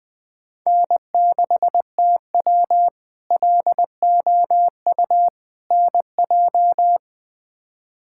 Morse code, heard as a keyed tone, about 20 wpm, 700 Hz. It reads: N6TW LOU NJ
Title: K1USN SST QSO Practice - 20wpm